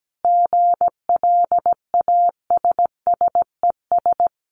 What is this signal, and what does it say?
Morse code, heard as a keyed tone, about 17 wpm, 700 Hz. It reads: GLASSES
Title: Additional Easy Phrases 17wpm